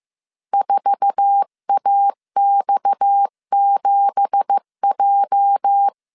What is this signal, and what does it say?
Morse code, heard as a keyed tone, about 15 wpm, 800 Hz. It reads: 4AX7J
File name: CW / DET / 1 9 1500C S B 14wpm 3302